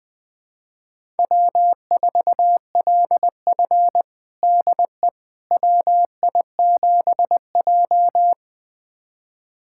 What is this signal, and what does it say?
Morse code, heard as a keyed tone, about 20 wpm, 700 Hz. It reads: W4LF DE WI7J
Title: USA Callsign DE Callsign - 4 Characters - 20wpm